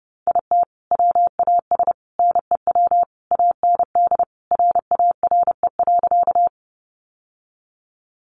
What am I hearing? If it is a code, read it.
Morse code, heard as a keyed tone, about 30 wpm, 700 Hz. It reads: IT WAS NEW AND RARE.
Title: Sentences from Top 1100 Words 30wpm